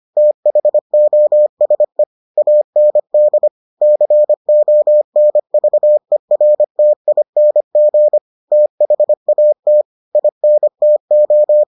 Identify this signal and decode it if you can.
Morse code, heard as a keyed tone, about 25 wpm, 600 Hz. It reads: THOSE AND CONVERTING THAT INTO